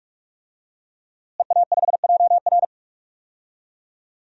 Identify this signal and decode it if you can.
Morse code, heard as a keyed tone, about 41 wpm, 700 Hz. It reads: EA5JR
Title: DXCC-100 Call Signs - 5 Characters - Repeated 2x - 45wpm